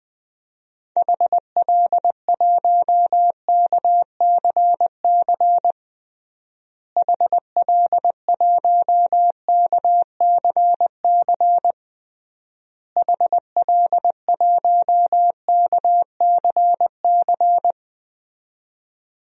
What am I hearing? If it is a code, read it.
Morse code, heard as a keyed tone, about 20 wpm, 700 Hz. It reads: HL1KCC HL1KCC HL1KCC
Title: DXCC-100 Call Signs - 6 Characters - Repeated 3x - 20wpm